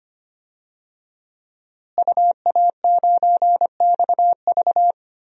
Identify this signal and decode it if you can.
Morse code, heard as a keyed tone, about 25 wpm, 700 Hz. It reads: UA9XV
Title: DXCC-100 Call Signs - 5 Characters - Repeated 3x - 25wpm